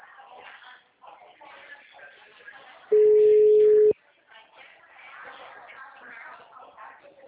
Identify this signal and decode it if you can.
Ringback tone (Europe, 425 Hz), with speech.
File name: out- -2012- - - .942